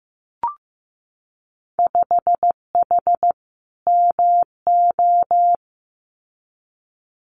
Morse code, heard as a keyed tone, about 15 wpm, 700 Hz. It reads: 5H MO
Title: Winter Field Day Exchanges Encore 15wpm